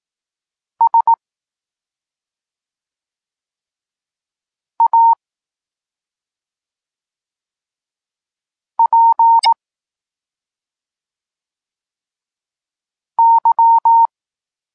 Morse code, heard as a keyed tone, about 18 wpm, 920 Hz. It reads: S A P Y
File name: Tekst-34 4